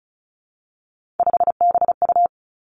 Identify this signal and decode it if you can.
Morse code, heard as a keyed tone, about 36 wpm, 700 Hz. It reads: 5BU
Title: Three Challenging Letter-Number Combinations 35wpm